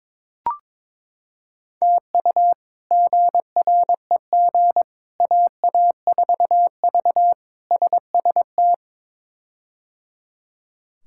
Morse code, heard as a keyed tone, about 22 wpm, 700 Hz. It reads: TU GREG AA4V SST
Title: K1USN SST QSO Practice - 22wpm